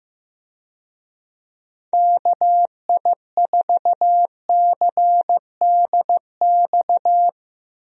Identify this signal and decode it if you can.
Morse code, heard as a keyed tone, about 15 wpm, 700 Hz. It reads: KI4CDX
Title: USA Call Signs - 6 Characters - Repeated 2x - 15wpm